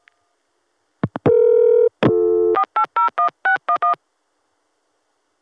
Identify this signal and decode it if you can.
DTMF, heard as telephone touch tones, tones of about 100 ms. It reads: 4801611